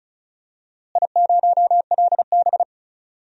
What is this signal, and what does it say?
Morse code, heard as a keyed tone, about 36 wpm, 700 Hz. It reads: I0LB
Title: DXCC-100 Call Signs - 4 Characters - Repeated 2x - 35wpm